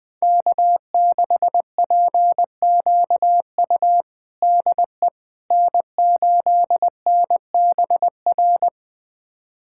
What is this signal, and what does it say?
Morse code, heard as a keyed tone, about 20 wpm, 700 Hz. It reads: K6PQU DE N8NBR
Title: USA Callsign DE Callsign - 5 Characters - 20wpm